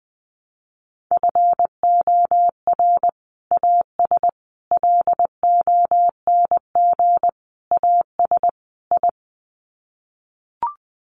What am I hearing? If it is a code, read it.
Morse code, heard as a keyed tone, about 20 wpm, 700 Hz. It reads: FOR AS LONG AS I
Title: Sets of 5 Words Encore2 20wpm